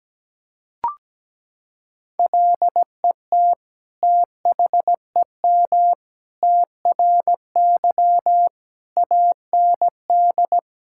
Morse code, heard as a keyed tone, about 17 wpm, 700 Hz. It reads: LET THEM TRY AND
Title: Sentences from Top 1000 Words - Review Part 5 - 17wpm